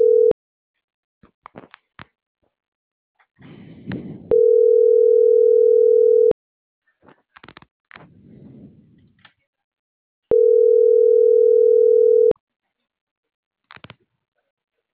Ringback tone (North America), 440 + 480 Hz.